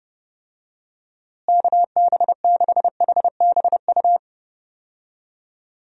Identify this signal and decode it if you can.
Morse code, heard as a keyed tone, about 30 wpm, 700 Hz. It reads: KB6HBU
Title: USA Call Signs - 6 Characters - Repeated 2x - 30wpm